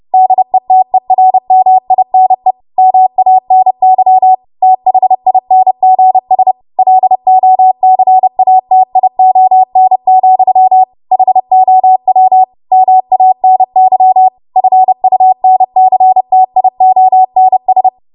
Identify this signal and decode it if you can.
Morse code, heard as a keyed tone, about 30 wpm, 750 Hz. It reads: DETERMINE MANY THINGS LOCATION, HOW MANY FUNCTIONS
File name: Morse / 30WPM